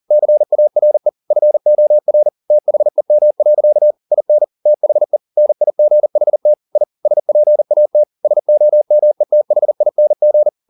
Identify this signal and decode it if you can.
Morse code, heard as a keyed tone, about 40 wpm, 600 Hz. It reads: CARE FOR THEM. IN THE NIGHT I SPAT SOMETHING
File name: the old man and the sea 2 40wpm